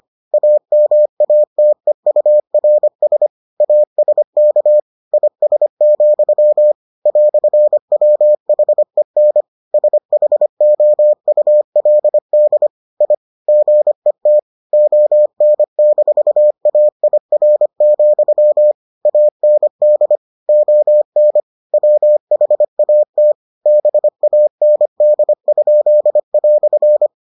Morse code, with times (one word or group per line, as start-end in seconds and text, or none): 0.3-3.3 AMATEURS
3.6-4.8 ASK
5.1-6.7 IS,
7.1-9.4 "WHEN
9.7-12.7 SHOULD
13.0-13.1 I
13.5-14.4 GET
14.7-18.7 ON-AIR,
19.1-20.2 AND
20.5-21.4 ON
21.7-23.3 WHAT
23.7-27.1 BAND?"